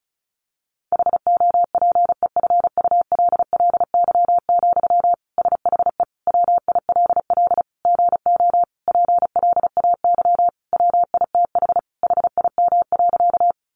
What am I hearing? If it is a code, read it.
Morse code, heard as a keyed tone, about 34 wpm, 700 Hz. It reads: HOPEFULLY, SHE WILL GO PLAY WITH HIM.
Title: Sentences from Top 1500 Words 35wpm